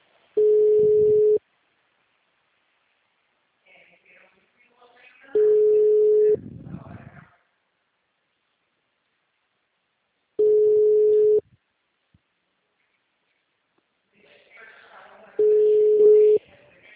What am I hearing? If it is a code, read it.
Ringback tone (Europe, 425 Hz), with speech.
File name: out- -2040- - - .66